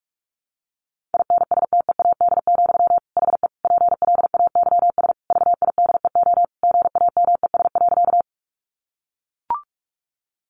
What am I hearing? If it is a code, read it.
Morse code, heard as a keyed tone, about 44 wpm, 700 Hz. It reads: INSTEAD, HE PLAYS VIDEO GAMES.
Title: Additional Easy Phrases 45wpm